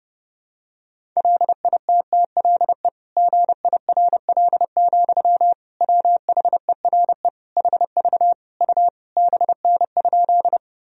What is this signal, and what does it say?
Morse code, heard as a keyed tone, about 30 wpm, 700 Hz. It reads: LITTLE GIRL, WHERE HV U BN?
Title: Sentences from Top 400 Words and Abbreviations 30wpm